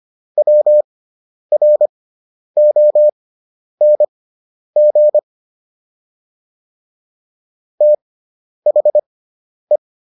Morse code, heard as a keyed tone, about 25 wpm, spaced as for 10 wpm, 600 Hz. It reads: WRONG THE
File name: SS213 10